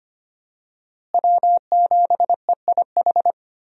Morse code, heard as a keyed tone, about 25 wpm, 700 Hz. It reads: W7EIH
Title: USA Call Signs - 5 Characters - Repeated 2x - 25wpm